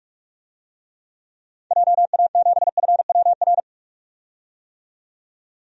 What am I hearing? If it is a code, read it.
Morse code, heard as a keyed tone, about 41 wpm, 700 Hz. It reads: JA7FWR
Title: DXCC-100 Call Signs - 6 Characters - 45wpm